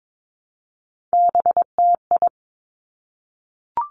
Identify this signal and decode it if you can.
Morse code, heard as a keyed tone, about 22 wpm, 700 Hz. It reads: BTI
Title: Three Challenging Letter-Number Combinations 22wpm